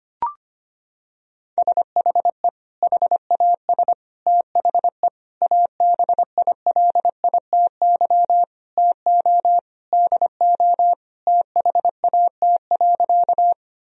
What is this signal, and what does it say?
Morse code, heard as a keyed tone, about 25 wpm, 700 Hz. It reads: SHE HAS THE ABILITY TO DO THAT.
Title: Sentences from Top 1000 Words - Review Part 3 - 25wpm